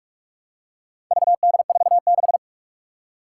Morse code, heard as a keyed tone, about 46 wpm, 700 Hz. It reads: VD46
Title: Four Challenging Letter-Number Combinations 45wpm